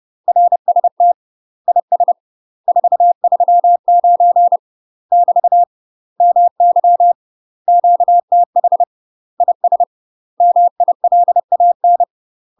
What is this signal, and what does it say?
Morse code, heard as a keyed tone, about 30 wpm, 700 Hz. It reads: RST IS 439 <BT> MY QTH IS MILAN
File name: UK MASTER QSOs TRAINING / 30WPM / UK MASTER QSOs 0033